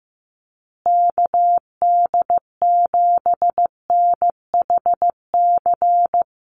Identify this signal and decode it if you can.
Morse code, heard as a keyed tone, about 15 wpm, 700 Hz. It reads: KD7NHC
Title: USA Call Signs Repeated 3x 15wpm